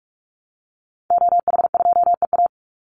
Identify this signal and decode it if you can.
Morse code, heard as a keyed tone, about 44 wpm, 700 Hz. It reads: OH2EA
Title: Call Signs Repeated 3x - 45wpm